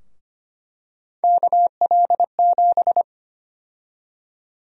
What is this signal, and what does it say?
Morse code, heard as a keyed tone, about 25 wpm, 700 Hz. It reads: KL7
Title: USA Call Sign Prefixes - Repeated 2x - 25wpm